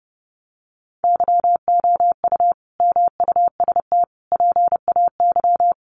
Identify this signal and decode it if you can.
Morse code, heard as a keyed tone, about 30 wpm, 700 Hz. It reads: YOU MUST PAY
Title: Sentences from Top 1400 Words 30wpm